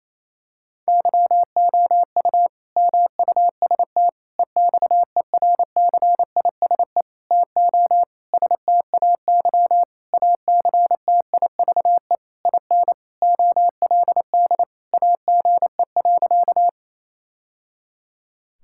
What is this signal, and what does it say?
Morse code, heard as a keyed tone, about 28 wpm, 700 Hz. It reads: YOU MUST EXERCISE TO STAY ACTIVE IN OLD AGE.
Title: Sentences from Top 1000 Words - Review Part 6 - 28wpm